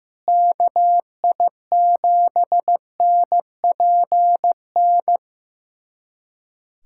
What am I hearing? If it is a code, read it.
Morse code, heard as a keyed tone, about 15 wpm, 700 Hz. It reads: KI7NPN
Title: USA Callsign DE Callsign 15wpm